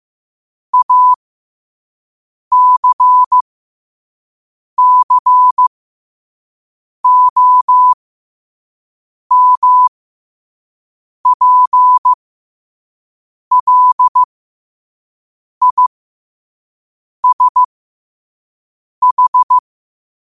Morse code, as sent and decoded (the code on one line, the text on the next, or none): .- -.-. -.-. --- -- .--. .-.. .. ... ....
ACCOMPLISH